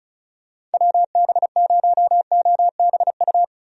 Morse code, heard as a keyed tone, about 36 wpm, 700 Hz. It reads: WB0OBU
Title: USA Call Signs - 6 Characters - Repeated 3x - 35wpm